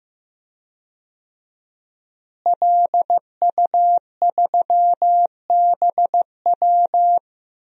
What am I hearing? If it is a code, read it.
Morse code, heard as a keyed tone, about 15 wpm, 700 Hz. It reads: LU3BW
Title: DXCC-100 Call Signs - 5 Characters - Repeated 3x - 15wpm